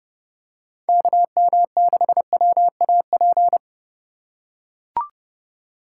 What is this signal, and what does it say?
Morse code, heard as a keyed tone, about 28 wpm, 700 Hz. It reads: KM6WAP